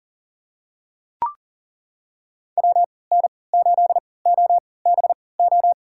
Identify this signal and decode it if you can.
Morse code, heard as a keyed tone, about 40 wpm, spaced as for 23 wpm, 700 Hz. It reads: WN8OBO
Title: USA Call Signs - 3x Character Spacing - 40wpm